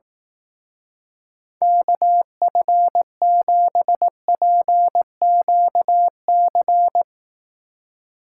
Morse code, heard as a keyed tone, about 18 wpm, 700 Hz. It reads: KF7PQC
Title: Speed Racing - USA 2x3 Call Signs 15wpm